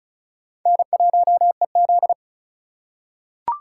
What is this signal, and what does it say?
Morse code, heard as a keyed tone, about 36 wpm, 700 Hz. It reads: N1EZ
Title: USA Call Signs - 4 Characters - 4x Character Spacing - 35wpm